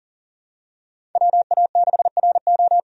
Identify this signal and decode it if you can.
Morse code, heard as a keyed tone, about 37 wpm, 700 Hz. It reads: WA6RO
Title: USA Call Signs - 5 Characters - 40wpm